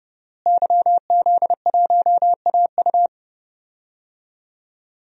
Morse code, heard as a keyed tone, about 30 wpm, 700 Hz. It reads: YZ1AU
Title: DXCC-100 Call Signs - 5 Characters - Repeated 2x - 30wpm